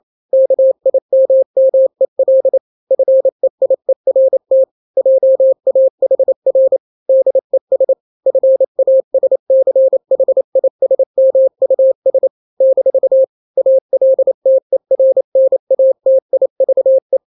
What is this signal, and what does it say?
Morse code, heard as a keyed tone, about 28 wpm, 530 Hz. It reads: KIMMEL FEIERT JAHR DES FASCHISMUS - ALTERNATIVE
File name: daily / 27wpm